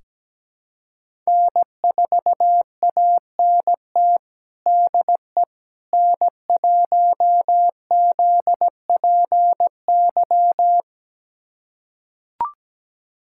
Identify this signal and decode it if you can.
Morse code, heard as a keyed tone, about 17 wpm, 700 Hz. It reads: N4ANT DE N1ZPY
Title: USA Callsign DE Callsign - 5 Characters - 17wpm